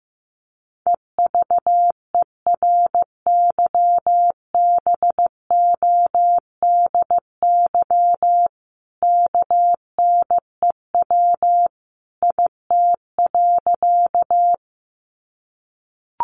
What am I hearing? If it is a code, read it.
Morse code, heard as a keyed tone, about 15 wpm, 700 Hz. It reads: EVERYBODY KNEW IT.
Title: Sentences from Top 1400 Words 15wpm